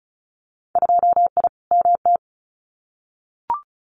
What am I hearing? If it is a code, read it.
Morse code, heard as a keyed tone, about 34 wpm, 700 Hz. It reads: 2I MT
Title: Winter Field Day Exchanges Encore 35wpm